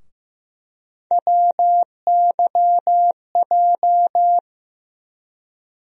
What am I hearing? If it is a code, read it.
Morse code, heard as a keyed tone, about 15 wpm, 700 Hz. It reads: WYJ
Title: USA Call Sign Postfixes - 3 Letters 15wpm